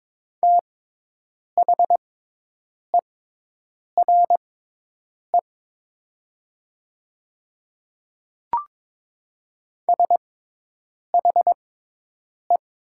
Morse code, heard as a keyed tone, about 22 wpm, spaced as for 8 wpm, 700 Hz. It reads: THERE SHE
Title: 6x Character Spacing - Top 100 Words - 22wpm